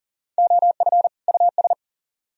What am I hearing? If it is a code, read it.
Morse code, heard as a keyed tone, about 49 wpm, 700 Hz. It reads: OF US
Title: Speed-Racing - Sets of 2 Words 40wpm